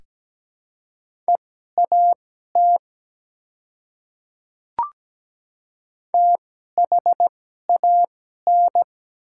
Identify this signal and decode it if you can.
Morse code, heard as a keyed tone, about 17 wpm, spaced as for 12 wpm, 690 Hz. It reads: EAT THAN
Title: 2x Character Spacing - Top 300 Words - 17wpm